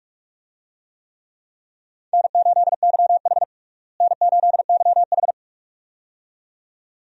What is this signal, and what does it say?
Morse code, heard as a keyed tone, about 45 wpm, 700 Hz. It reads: N8YH N8YH
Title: USA Call Signs - 4 Characters - Repeated 2x - 45wpm